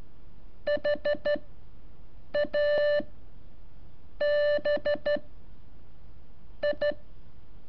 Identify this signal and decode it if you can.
Morse code, heard as a keyed tone, about 13 wpm, spaced as for 6 wpm, 590 Hz. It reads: HABI
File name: curso / ejercicio 18